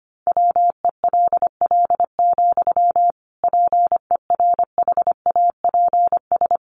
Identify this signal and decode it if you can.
Morse code, heard as a keyed tone, about 25 wpm, 700 Hz. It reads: WELL, PERHAPS
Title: Sentences from Top 500 Words 25wpm